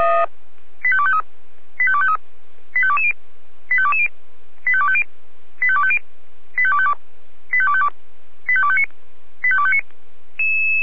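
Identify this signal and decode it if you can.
Five-tone selective call (ZVEI), heard as quick runs of five tones, tones of about 70 ms. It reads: 85253 85253 85220 85220 85250 85250 85252 85252 85259 85259
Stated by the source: Five-tone ZVEI selective call received by SDR.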